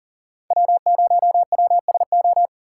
Morse code, heard as a keyed tone, about 40 wpm, 700 Hz. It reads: W0WSO
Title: USA Call Signs - 4x Character Spacing - 40wpm